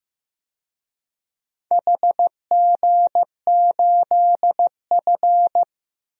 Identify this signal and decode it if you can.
Morse code, heard as a keyed tone, about 15 wpm, 700 Hz. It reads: HG8F
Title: DXCC-100 Call Signs - 4 Characters - 15wpm